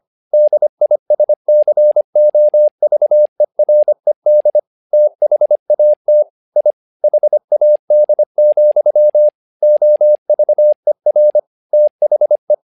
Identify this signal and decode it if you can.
Morse code, heard as a keyed tone, about 25 wpm, 600 Hz. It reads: DISCOVERED THAT I HAD, OVER THE
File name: learning center / tower trouble / tower trouble 25wpm